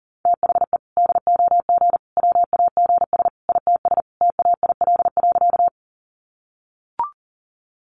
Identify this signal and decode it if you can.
Morse code, heard as a keyed tone, about 40 wpm, 700 Hz. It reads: THE DOG WAGS ITS TAIL.
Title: Easy Phrases 40wpm